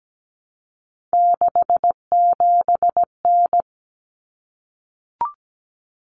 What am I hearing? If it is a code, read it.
Morse code, heard as a keyed tone, about 17 wpm, 700 Hz. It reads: 67N
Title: Common Cut Numbers - 3-Digit 17wpm